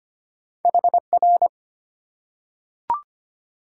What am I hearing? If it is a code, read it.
Morse code, heard as a keyed tone, about 25 wpm, 700 Hz. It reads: HR
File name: Top 90 QSO Elements - 8x Character Spacing - 25wpm